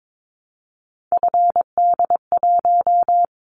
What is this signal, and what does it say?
Morse code, heard as a keyed tone, about 22 wpm, 700 Hz. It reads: FD1
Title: Callsign Prefixes for Top 50 DXCC - 22wpm